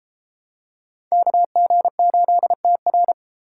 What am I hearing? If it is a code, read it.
Morse code, heard as a keyed tone, about 30 wpm, 700 Hz. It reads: KG8TR